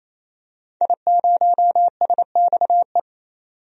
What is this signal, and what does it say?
Morse code, heard as a keyed tone, about 28 wpm, 700 Hz. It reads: I0SXE
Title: DXCC-100 Call Signs - 5 Characters - 28wpm